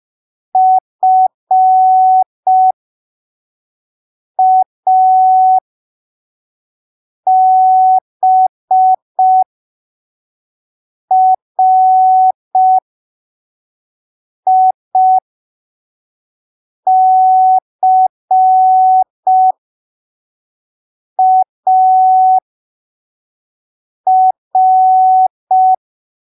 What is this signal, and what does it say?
Morse code, heard as a keyed tone, about 5 wpm, spaced as for 3 wpm, 750 Hz. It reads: FABRICAR
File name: arquivos / texto / texto 20 05ppm